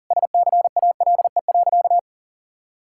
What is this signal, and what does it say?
Morse code, heard as a keyed tone, about 40 wpm, 700 Hz. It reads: SCALE.
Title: Sentences from Top 1000 Words - Review Part 5 - 40wpm